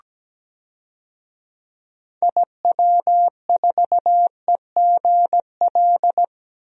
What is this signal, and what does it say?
Morse code, heard as a keyed tone, about 17 wpm, 700 Hz. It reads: IW4EGL
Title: DXCC-100 Call Signs - 6 Characters - 17wpm